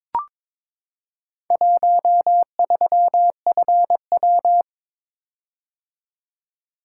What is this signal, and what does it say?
Morse code, heard as a keyed tone, about 22 wpm, 700 Hz. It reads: 13FW
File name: practice / 22wpm / Four Letter-Number Combinations - 22wpm